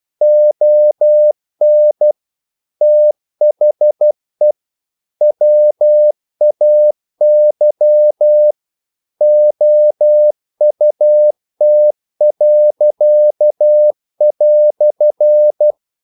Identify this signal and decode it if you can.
Morse code, heard as a keyed tone, about 12 wpm, 600 Hz. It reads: ON THE WAY OUT."
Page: English quotations in CW (telegraphy - morse)